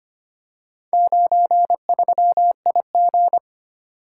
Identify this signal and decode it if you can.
Morse code, heard as a keyed tone, about 25 wpm, 700 Hz. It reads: 93IG
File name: practice / 25wpm / Four Letter-Number Combinations - 25wpm